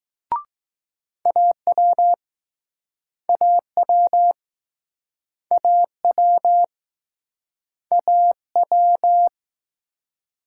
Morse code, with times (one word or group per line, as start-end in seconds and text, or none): 1.3-2.1 AW
3.3-4.3 AW
5.5-6.6 AW
7.9-8.3 A
8.6-9.3 W